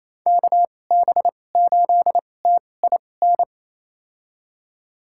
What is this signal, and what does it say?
Morse code, heard as a keyed tone, about 28 wpm, spaced as for 20 wpm, 700 Hz. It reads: KB8TIN
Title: USA Call Signs - 6 Characters - 2x Character Spacing - 28wpm